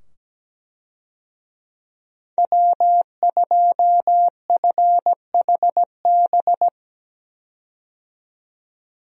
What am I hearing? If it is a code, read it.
Morse code, heard as a keyed tone, about 17 wpm, 700 Hz. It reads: W2FHB
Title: USA Call Signs - 5 Characters - 17wpm